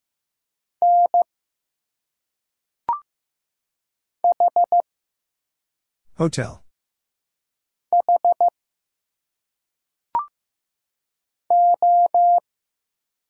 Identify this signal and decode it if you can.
Morse code, heard as a keyed tone, about 15 wpm, 700 Hz. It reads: N H H O
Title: Single Letters - Phonetic Pronunciation 15wpm